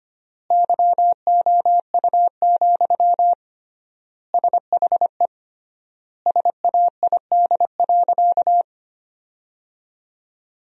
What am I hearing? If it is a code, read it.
Morse code, heard as a keyed tone, about 25 wpm, 700 Hz. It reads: YOU, SHE SAID.
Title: 3x Word-Spacing - Sentences from Top 400 Words - 25wpm